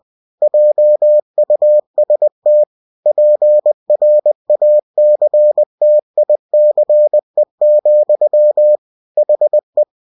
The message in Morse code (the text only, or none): JUST PRACTICE, HE